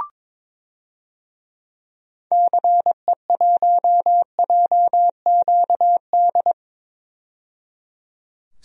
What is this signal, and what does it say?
Morse code, heard as a keyed tone, about 23 wpm, 700 Hz. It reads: CE1JQD
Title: DXCC-100 Call Signs - 6 Characters - 22wpm